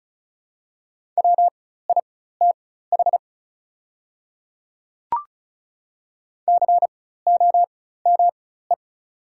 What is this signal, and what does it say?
Morse code, heard as a keyed tone, about 36 wpm, spaced as for 16 wpm, 690 Hz. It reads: WITH COME
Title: 4x Character Spacing - Top 100 Words - 35wpm